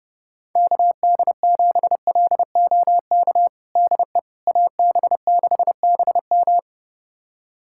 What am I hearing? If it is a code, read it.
Morse code, heard as a keyed tone, about 32 wpm, 700 Hz. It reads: KD7LOK DE AB6BM
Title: USA Callsign DE Callsign 30wpm